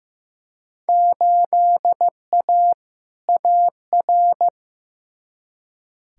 Morse code, heard as a keyed tone, about 15 wpm, 700 Hz. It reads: 8A AR
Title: Field Day Exchanges - 15wpm